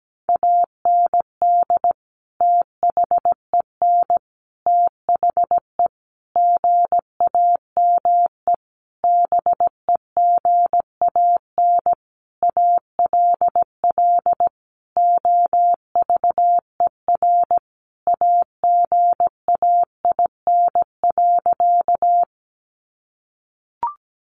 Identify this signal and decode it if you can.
Morse code, heard as a keyed tone, about 17 wpm, 700 Hz. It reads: AND THEN THE GAME BEGAN ALL OVER AGAIN.
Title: Sentences from Top 500 Words 17wpm